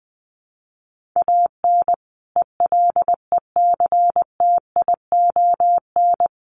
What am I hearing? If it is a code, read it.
Morse code, heard as a keyed tone, about 20 wpm, 700 Hz. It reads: AN ELECTION
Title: Sentences from Top 1500 Words 20wpm